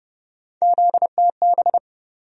Morse code, heard as a keyed tone, about 30 wpm, 700 Hz. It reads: ZTB
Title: USA Call Sign Postfixes - 3 Letters 30wpm